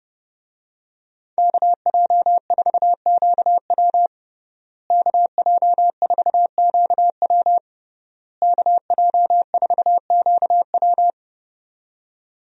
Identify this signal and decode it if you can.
Morse code, heard as a keyed tone, about 32 wpm, 700 Hz. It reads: KJ4QW KJ4QW KJ4QW
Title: USA Call Signs - 5 Characters - Repeated 3x - 30wpm